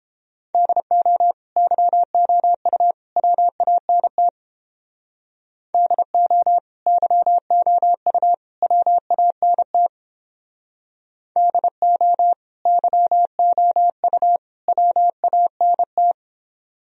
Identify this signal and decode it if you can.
Morse code, heard as a keyed tone, about 29 wpm, 700 Hz. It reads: DO YOU WANT DO YOU WANT DO YOU WANT